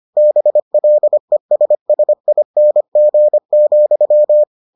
Morse code, heard as a keyed tone, about 25 wpm, 600 Hz. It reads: BLESSING,